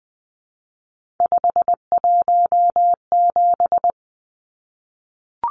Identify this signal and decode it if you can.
Morse code, heard as a keyed tone, about 20 wpm, 700 Hz. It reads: 517
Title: Numbers - 3-Digit 20wpm